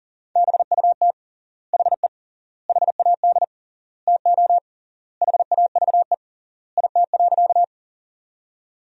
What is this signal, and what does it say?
Morse code, heard as a keyed tone, about 37 wpm, 700 Hz. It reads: BUT HE HAD TO HAVE IT.
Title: 3x Word-Spacing - Sentences from Top 100 Words - 40wpm